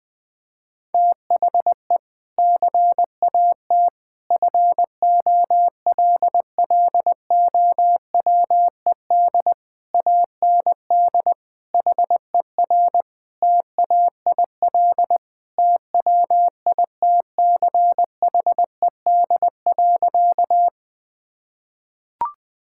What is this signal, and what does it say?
Morse code, heard as a keyed tone, about 20 wpm, 700 Hz. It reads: THE CAT FOLLOWED AND HER TAIL TWITCHED.
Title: Sentences with 2-Word Phrases - 20wpm